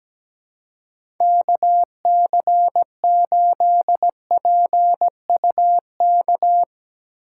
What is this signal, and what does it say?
Morse code, heard as a keyed tone, about 17 wpm, 700 Hz. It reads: KC8PUK
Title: USA Call Signs - 2x Character Spacing - 17wpm